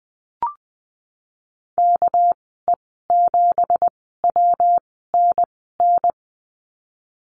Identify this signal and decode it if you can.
Morse code, heard as a keyed tone, about 20 wpm, spaced as for 15 wpm, 700 Hz. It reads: KE7WNN
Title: 219 Lesson 35g - Number 0 - Call Signs 20wpm